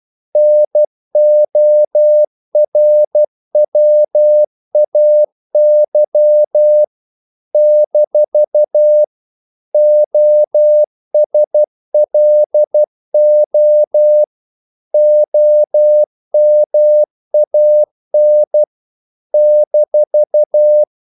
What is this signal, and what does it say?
Morse code, heard as a keyed tone, about 12 wpm, 600 Hz. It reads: NORWAY - OSLO OMAN -
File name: CW / Practice / NationalCapitals12